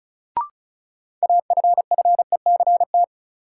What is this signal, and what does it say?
Morse code, heard as a keyed tone, about 36 wpm, 700 Hz. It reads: AFFECT
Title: Sentences from Top 900 Words - New Word Drill - 35wpm